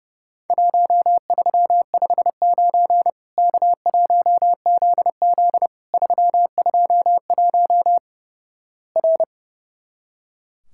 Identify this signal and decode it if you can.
Morse code, heard as a keyed tone, about 32 wpm, 700 Hz. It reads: 1359 K1ZZ 321 R
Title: Worked All Europe DX Contest - QTC Exchange - 30wpm